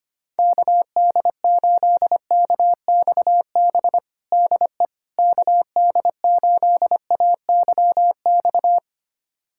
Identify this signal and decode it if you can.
Morse code, heard as a keyed tone, about 25 wpm, 700 Hz. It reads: KD8KXB DE KD8AYX
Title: USA Callsign DE Callsign 25wpm